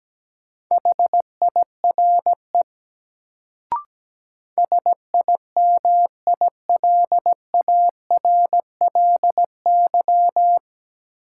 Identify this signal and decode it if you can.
Morse code, heard as a keyed tone, about 17 wpm, 700 Hz. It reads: HIRE SIMILARLY
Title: Sentences from Top 1400 Words - New Word Drill Alt - 17wpm